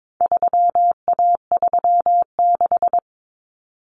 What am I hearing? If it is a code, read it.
Morse code, heard as a keyed tone, about 22 wpm, 700 Hz. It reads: 3A36
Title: Common Cut Numbers - 4-Digit 22wpm